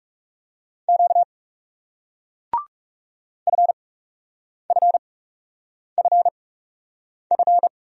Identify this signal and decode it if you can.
Morse code, heard as a keyed tone, about 40 wpm, spaced as for 8 wpm, 700 Hz. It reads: Q FFFF